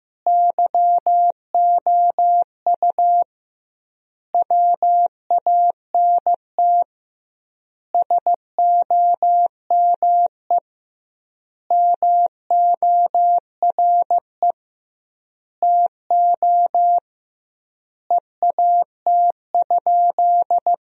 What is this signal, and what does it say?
Morse code, heard as a keyed tone, about 15 wpm, 700 Hz. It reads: YOU WANT SOME MORE TO EAT?
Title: 2x Word-Spacing - Sentences from Top 300 Words - 15wpm